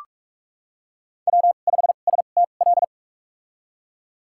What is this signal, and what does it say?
Morse code, heard as a keyed tone, about 44 wpm, spaced as for 32 wpm, 700 Hz. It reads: W5STL
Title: USA Call Signs - 5 Characters - 2x Character Spacing - 45wpm